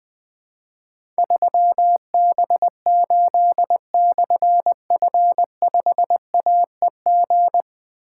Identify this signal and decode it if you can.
Morse code, heard as a keyed tone, about 20 wpm, 700 Hz. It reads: 3B8/F5AEG
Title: Difficult Call Signs 20wpm